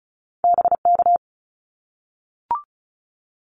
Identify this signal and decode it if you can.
Morse code, heard as a keyed tone, about 36 wpm, 700 Hz. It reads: BK
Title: Abbreviations 35wpm